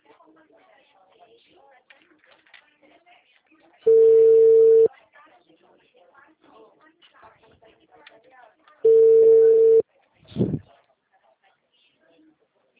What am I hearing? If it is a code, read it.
Ringback tone (China, 450 Hz), with speech.
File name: out- -2009- - - .72396